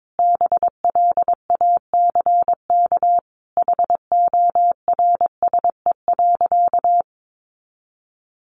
Morse code, heard as a keyed tone, about 22 wpm, 700 Hz. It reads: BLACK HORSE.
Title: Sentences from Top 1500 Words 22wpm